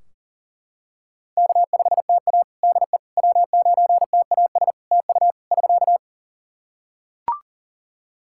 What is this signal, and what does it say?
Morse code, heard as a keyed tone, about 40 wpm, 700 Hz. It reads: K5TA DE W9TAS TU <SK>
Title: Practice QSO - Exchange 4 - Ending - 40wpm